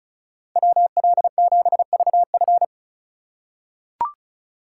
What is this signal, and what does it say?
Morse code, heard as a keyed tone, about 36 wpm, 700 Hz. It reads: WL7VF
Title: USA Call Signs - 5 Characters - 2x Character Spacing - 35wpm